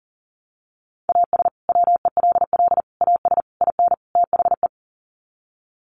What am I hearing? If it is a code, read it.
Morse code, heard as a keyed tone, about 40 wpm, 700 Hz. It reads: AS WELL AS IN THE
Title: Sets of 5 Words Encore 40wpm